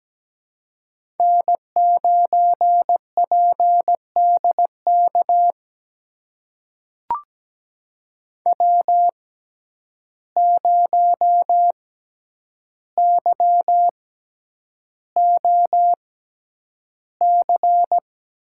Morse code, heard as a keyed tone, about 17 wpm, spaced as for 9 wpm, 700 Hz. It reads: N9PDK W0YOC
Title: USA Call Signs - 6x Character Spacing - 17wpm